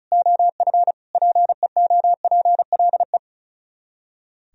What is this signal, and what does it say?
Morse code, heard as a keyed tone, about 44 wpm, 700 Hz. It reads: OF PEOPLE